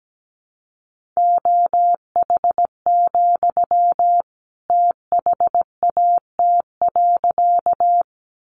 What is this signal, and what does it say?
Morse code, heard as a keyed tone, about 17 wpm, 700 Hz. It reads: OH, THAT.
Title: Sentences from Top 500 Words 17wpm